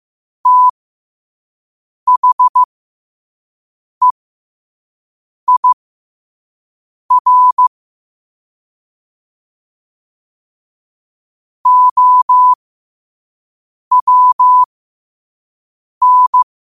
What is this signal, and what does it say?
Morse code, heard as a keyed tone, about 15 wpm, spaced as for 5 wpm, 1000 Hz. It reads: THEIR OWN
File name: qotdfiles / 5wpm